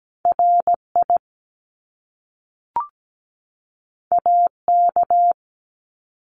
Morse code, heard as a keyed tone, about 17 wpm, 700 Hz. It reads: RI AK
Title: US States 17wpm